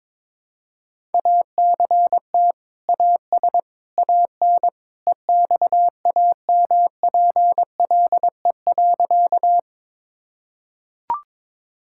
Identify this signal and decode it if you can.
Morse code, heard as a keyed tone, about 22 wpm, 700 Hz. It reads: ACT AS AN EXAMPLE.
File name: practice / 22wpm / Sentences with 2-Word Phrases - 22wpm